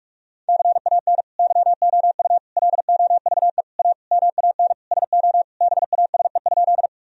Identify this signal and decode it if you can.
Morse code, heard as a keyed tone, about 41 wpm, 700 Hz. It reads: CAN YOU LOVE A MAN SO BASE?
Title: Sentences from Top 1000 Words - Review Part 1 - 45wpm